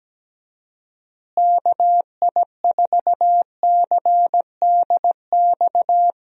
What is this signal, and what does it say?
Morse code, heard as a keyed tone, about 17 wpm, 700 Hz. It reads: KI4CDX
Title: USA Call Signs - 6 Characters - Repeated 2x - 17wpm